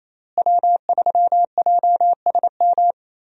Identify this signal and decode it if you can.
Morse code, heard as a keyed tone, about 28 wpm, 700 Hz. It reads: W3JSM
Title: USA Call Signs - 5 Characters - Repeated 3x - 28wpm